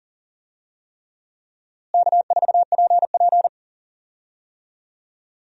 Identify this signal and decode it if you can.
Morse code, heard as a keyed tone, about 39 wpm, 700 Hz. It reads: K4PP
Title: USA Call Signs - 4 Characters - Repeated 3x - 40wpm